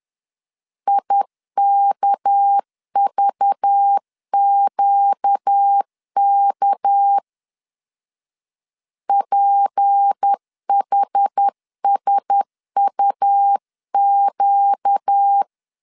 Morse code, heard as a keyed tone, about 10 wpm, 800 Hz. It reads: IKVQK PHSUQ
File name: LEZIONI / 6 PKX0 / 6 PK 10wpm 2436